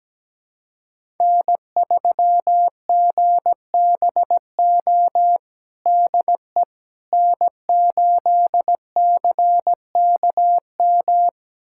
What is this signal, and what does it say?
Morse code, heard as a keyed tone, about 17 wpm, 700 Hz. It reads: N3GBO DE N8CKM
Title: USA Callsign DE Callsign - 5 Characters - 17wpm